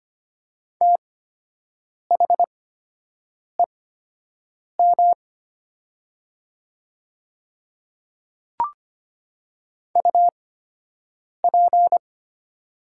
Morse code, heard as a keyed tone, about 25 wpm, spaced as for 7 wpm, 700 Hz. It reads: THEM UP
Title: 8x Character Spacing - Top 100 Words - 25wpm